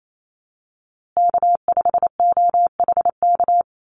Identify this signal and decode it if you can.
Morse code, heard as a keyed tone, about 28 wpm, 700 Hz. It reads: K5OHK
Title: USA Call Signs Repeated 2x 28wpm